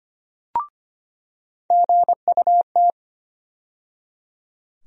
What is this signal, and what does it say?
Morse code, heard as a keyed tone, about 25 wpm, 700 Hz. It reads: GUT
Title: USA Call Sign Postfixes - 3 Letters 25wpm